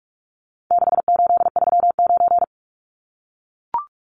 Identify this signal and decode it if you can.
Morse code, heard as a keyed tone, about 44 wpm, 700 Hz. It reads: 6839
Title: Numbers - 4-Digit 45wpm